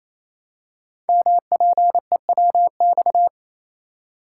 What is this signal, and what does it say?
Morse code, heard as a keyed tone, about 29 wpm, 700 Hz. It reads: MPEWX
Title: Code Groups - 5 Letters - 28wpm